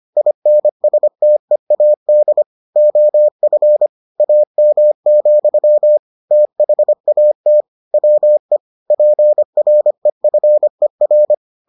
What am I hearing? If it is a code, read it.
Morse code, heard as a keyed tone, about 25 wpm, 600 Hz. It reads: INSTEAD OF AM, THAT WE PREFER